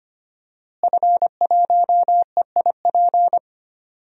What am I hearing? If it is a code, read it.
Morse code, heard as a keyed tone, about 25 wpm, 700 Hz. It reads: F1EIP
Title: DXCC-100 Call Signs - 5 Characters - Repeated 3x - 25wpm